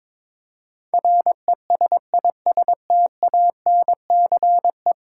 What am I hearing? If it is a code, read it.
Morse code, heard as a keyed tone, about 22 wpm, 700 Hz. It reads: RESISTANCE
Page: morse-code-ninja-practice-22wpm